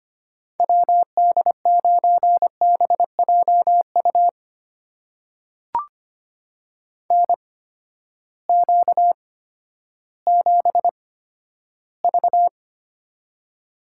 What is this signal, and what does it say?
Morse code, heard as a keyed tone, about 25 wpm, spaced as for 12 wpm, 700 Hz. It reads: WD9BJU NQ7V
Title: USA Call Signs - 8x Character Spacing - 25wpm